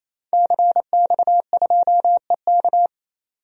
Morse code, heard as a keyed tone, about 28 wpm, 700 Hz. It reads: CX2EK
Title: DXCC-100 Call Signs - 5 Characters - 28wpm